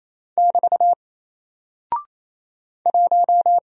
Morse code, heard as a keyed tone, about 27 wpm, 700 Hz. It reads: <BT> 1
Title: Single Character (Letter:Number:Punctuation:Prosign) Alt 28wpm